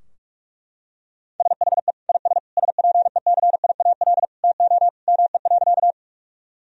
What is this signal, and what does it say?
Morse code, heard as a keyed tone, about 41 wpm, 700 Hz. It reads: SHE IS SPECIAL TO ME.
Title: Sentences from Top 1000 Words - Review Part 3 - 45wpm